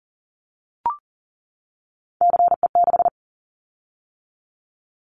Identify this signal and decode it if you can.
Morse code, heard as a keyed tone, about 40 wpm, 700 Hz. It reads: CE6
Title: Callsign Prefixes for Top 100 DXCC - 40wpm